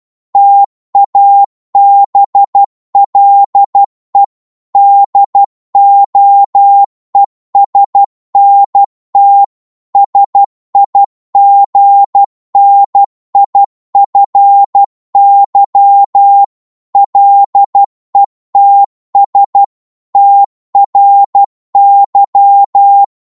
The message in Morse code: TABLE DOESNT SIGNIFY LETS TRY